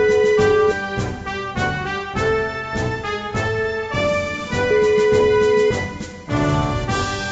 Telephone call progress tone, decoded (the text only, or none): ringback tone